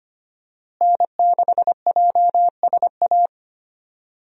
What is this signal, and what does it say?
Morse code, heard as a keyed tone, about 25 wpm, 700 Hz. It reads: N6JSA
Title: USA Call Signs - 5 Characters - Repeated 3x - 25wpm